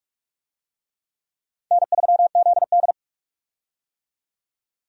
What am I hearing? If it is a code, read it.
Morse code, heard as a keyed tone, about 41 wpm, 700 Hz. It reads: N3ZD
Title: USA Call Signs - 4 Characters - 45wpm